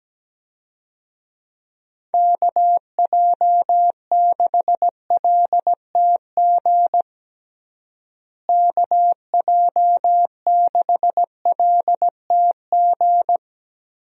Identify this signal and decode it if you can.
Morse code, heard as a keyed tone, about 17 wpm, 700 Hz. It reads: KJ6LTG KJ6LTG
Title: USA Call Signs - 6 Characters - Repeated 2x - 17wpm